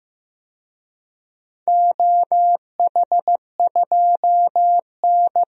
Morse code, heard as a keyed tone, about 15 wpm, 700 Hz. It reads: OH2N
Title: DXCC-100 Call Signs - 4 Characters - Repeated 2x - 15wpm